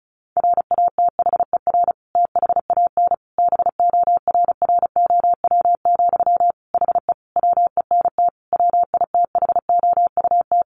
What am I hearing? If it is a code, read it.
Morse code, heard as a keyed tone, about 34 wpm, 700 Hz. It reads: RATHER THAN BORROW, HE WENT WITHOUT
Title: Sentences from Top 1500 Words 35wpm